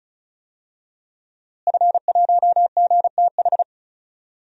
Morse code, heard as a keyed tone, about 36 wpm, 700 Hz. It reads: F1GTH
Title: DXCC-100 Call Signs - 5 Characters - Repeated 3x - 35wpm